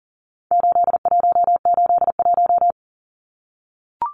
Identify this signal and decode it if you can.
Morse code, heard as a keyed tone, about 40 wpm, 700 Hz. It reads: 8181
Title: Numbers - 4-Digit 40wpm